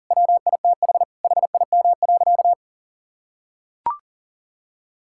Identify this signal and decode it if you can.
Morse code, heard as a keyed tone, about 40 wpm, 700 Hz. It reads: WITH HIM.
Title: Sentences from Top 1600 Words 40wpm